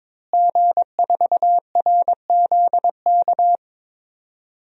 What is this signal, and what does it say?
Morse code, heard as a keyed tone, about 22 wpm, 700 Hz. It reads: G4RZK
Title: DXCC-100 Call Signs - 5 Characters - Repeated 3x - 22wpm